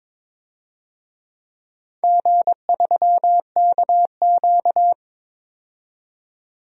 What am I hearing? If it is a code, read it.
Morse code, heard as a keyed tone, about 22 wpm, 700 Hz. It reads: G3KQ
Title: DXCC-100 Call Signs - 4 Characters - 22wpm